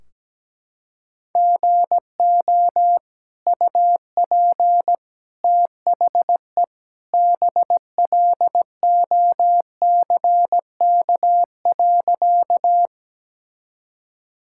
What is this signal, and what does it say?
Morse code, heard as a keyed tone, about 17 wpm, 700 Hz. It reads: GO UP THE BLOCK.
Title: Sentences from Top 1600 Words 17wpm